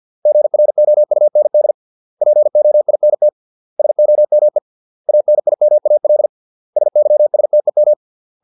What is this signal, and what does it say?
Morse code, heard as a keyed tone, about 50 wpm, 600 Hz. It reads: GROUND POINT SOME ANIMAL SYSTEM